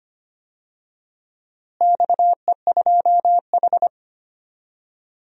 Morse code, heard as a keyed tone, about 25 wpm, 700 Hz. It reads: XE2H
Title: DXCC-100 Call Signs - 4 Characters - Repeated 2x - 25wpm